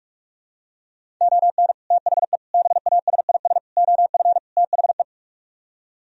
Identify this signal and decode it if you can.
Morse code, heard as a keyed tone, about 50 wpm, 700 Hz. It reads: ON THE BASIS OF THE